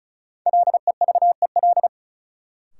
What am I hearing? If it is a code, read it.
Morse code, heard as a keyed tone, about 36 wpm, 700 Hz. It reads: LEVEL
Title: Sentences from Top 700 Words - New Word Drill - 35wpm